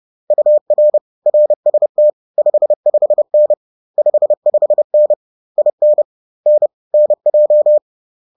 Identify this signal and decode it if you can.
Morse code, heard as a keyed tone, about 30 wpm, 600 Hz. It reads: UR RST 55N 55N IN N NJ
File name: 30wpm